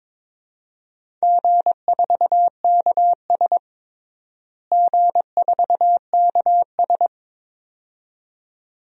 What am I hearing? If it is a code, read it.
Morse code, heard as a keyed tone, about 21 wpm, 700 Hz. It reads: G4KS G4KS
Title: DXCC-100 Call Signs - 4 Characters - Repeated 2x - 22wpm